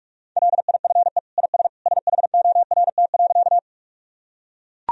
Morse code, heard as a keyed tone, about 41 wpm, 700 Hz. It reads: LIFE IS SHORT.
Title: Sentences from Top 1000 Words - Review Part 2 - 45wpm